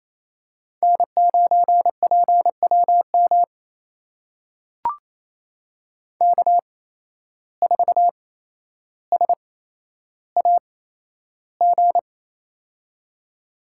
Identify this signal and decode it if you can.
Morse code, heard as a keyed tone, about 28 wpm, spaced as for 12 wpm, 700 Hz. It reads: N9PWM K4SAG
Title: USA Call Signs - 5 Characters - 8x Character Spacing - 28wpm